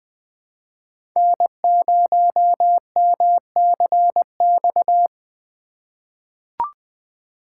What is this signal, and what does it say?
Morse code, heard as a keyed tone, about 20 wpm, 700 Hz. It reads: N0MCX
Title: USA Call Signs - 5 Characters - 3x Character Spacing - 20wpm